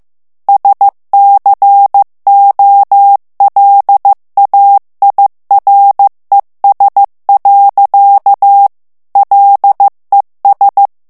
Morse code, heard as a keyed tone, about 15 wpm, 800 Hz. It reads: SCOLAIRES. LES
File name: morse mp3 / FRANCAIS TEXTE CLAIR 1 VITESSE 14 MOTS MINUTES